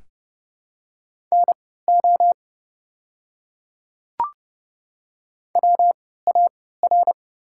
Morse code, heard as a keyed tone, about 30 wpm, spaced as for 17 wpm, 700 Hz. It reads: NO WAR
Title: 3x Character Spacing - Top 500 Words - 30wpm